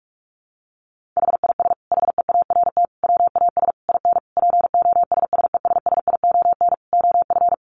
Morse code, heard as a keyed tone, about 44 wpm, 700 Hz. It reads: HIS HEART WAS IN POSSESSION OF
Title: Sentences from Top 1400 Words 45wpm